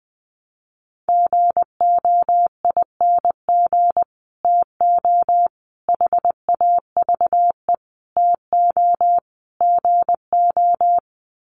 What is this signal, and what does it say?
Morse code, heard as a keyed tone, about 20 wpm, 700 Hz. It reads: GOING TO HAVE TO GO
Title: Sets of 5 Words Encore 20wpm